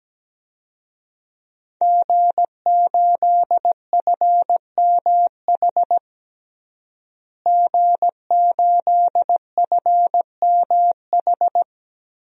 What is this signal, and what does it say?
Morse code, heard as a keyed tone, about 17 wpm, 700 Hz. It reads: G8FMH G8FMH
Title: DXCC-100 Call Signs - 5 Characters - Repeated 3x - 17wpm